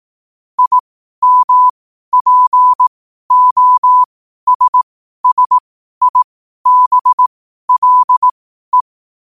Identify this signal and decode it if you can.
Morse code, heard as a keyed tone, about 18 wpm, spaced as for 12 wpm, 1000 Hz. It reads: IMPOSSIBLE
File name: qotdfiles / 12wpm